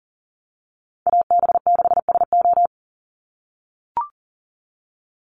Morse code, heard as a keyed tone, about 40 wpm, 700 Hz. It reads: AB6SO
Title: NA SOTA Chaser Callsigns 40wpm